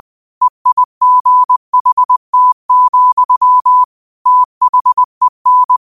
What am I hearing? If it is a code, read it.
Morse code, heard as a keyed tone, about 20 wpm, 1000 Hz. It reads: EIGHT, THEN